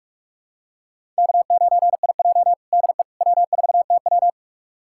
Morse code, heard as a keyed tone, about 42 wpm, 700 Hz. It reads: K9IJ DE W4TW
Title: USA Callsign DE Callsign - 4 Characters - 45wpm